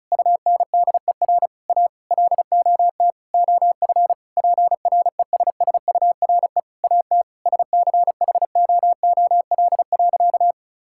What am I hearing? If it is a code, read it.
Morse code, heard as a keyed tone, about 34 wpm, 700 Hz. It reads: UNDER A LOT OF PRESSURE AT SCHOOL.
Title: Sentences from Top 1000 Words - Review Part 6 - 35wpm